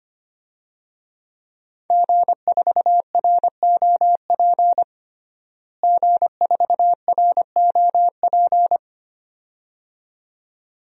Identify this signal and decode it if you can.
Morse code, heard as a keyed tone, about 25 wpm, 700 Hz. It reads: G4ROP G4ROP
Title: DXCC-100 Call Signs - 5 Characters - Repeated 2x - 25wpm